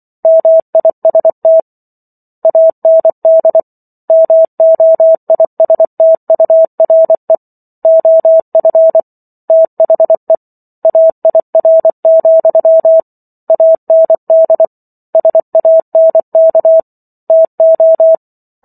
Morse code, heard as a keyed tone, about 24 wpm, 650 Hz. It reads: MIST AND MOISTURE OF THE AIR, AND SANK TO